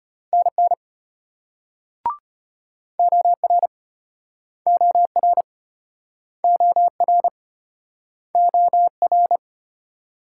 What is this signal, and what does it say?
Morse code, heard as a keyed tone, about 35 wpm, 700 Hz. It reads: NN OR OR OR OR